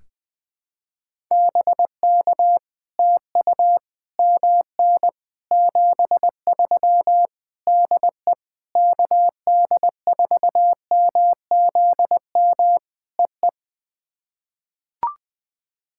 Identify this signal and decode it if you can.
Morse code, heard as a keyed tone, about 20 wpm, 700 Hz. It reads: BK TU MN 73 DE KD4MZM EE
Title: POTA QSO Practice - 20wpm